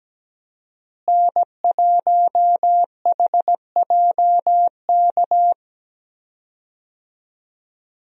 Morse code, heard as a keyed tone, about 17 wpm, 700 Hz. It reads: N1HJK
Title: USA Call Signs - 5 Characters - Repeated 2x - 17wpm